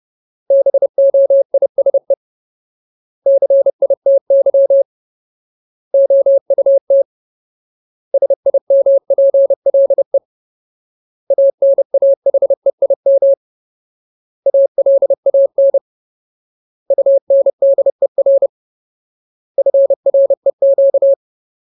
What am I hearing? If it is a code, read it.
Morse code, heard as a keyed tone, about 30 wpm, 550 Hz. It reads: BOISE CITY OUT SIMPLE ANAHEIM ALAN UNDER FREQ
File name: Ultimate Word List / 30WPM Ultimate